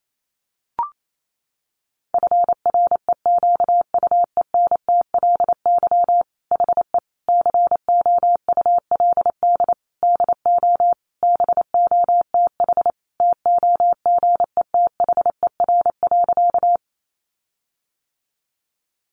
Morse code, heard as a keyed tone, about 28 wpm, 700 Hz. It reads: FREQUENTLY HE COULD DO BOTH TOGETHER.
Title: Sentences from Top 1300 Words 28wpm